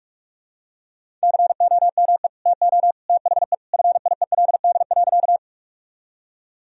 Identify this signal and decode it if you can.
Morse code, heard as a keyed tone, about 42 wpm, 700 Hz. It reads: COME TO THE FIELD.
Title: Sentences from Top 1000 Words - Review Part 2 - 45wpm